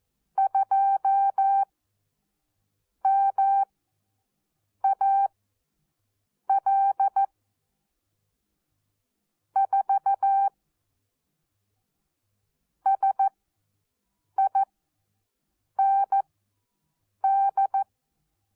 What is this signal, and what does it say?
Morse code, heard as a keyed tone, about 14 wpm, spaced as for 6 wpm, 780 Hz. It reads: 2 MAL 4 SIND
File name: Onlinelehrgang / morsen / DJ4UF Morsen14